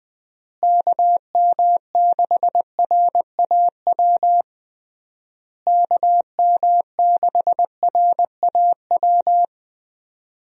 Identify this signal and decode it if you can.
Morse code, heard as a keyed tone, about 20 wpm, 700 Hz. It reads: KM6RAW KM6RAW
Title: USA Call Signs - 6 Characters - Repeated 3x - 20wpm